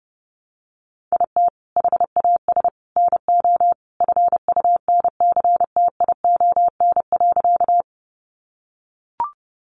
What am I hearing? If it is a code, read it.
Morse code, heard as a keyed tone, about 30 wpm, 700 Hz. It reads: IT HAS NO FUNCTION.
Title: Sentences from Top 1100 Words 30wpm